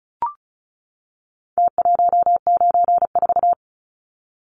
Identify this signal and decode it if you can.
Morse code, heard as a keyed tone, about 34 wpm, 700 Hz. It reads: T194
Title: Numbers - Contest Style - 4-Digit 35wpm